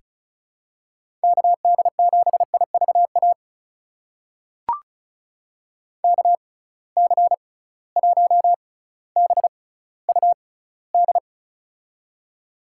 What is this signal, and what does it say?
Morse code, heard as a keyed tone, about 36 wpm, spaced as for 18 wpm, 700 Hz. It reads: KD7IVA KC1BUD
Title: USA Call Signs - 6 Characters - 6x Character Spacing - 35wpm